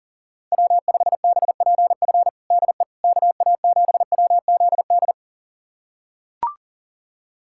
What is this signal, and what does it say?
Morse code, heard as a keyed tone, about 40 wpm, 700 Hz. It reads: W5BPF DE KA7WZD
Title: USA Callsign DE Callsign 40wpm